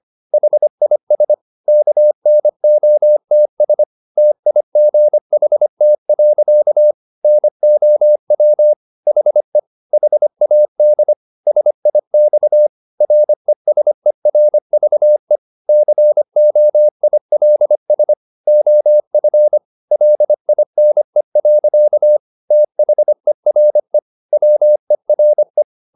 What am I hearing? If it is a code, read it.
Morse code, heard as a keyed tone, about 25 wpm, 600 Hz. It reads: HIS KNOTS TIGHT. NOW HE HAD SIX RESERVE COILS OF LINE. THERE WERE
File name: the old man and the sea part 1 25wpm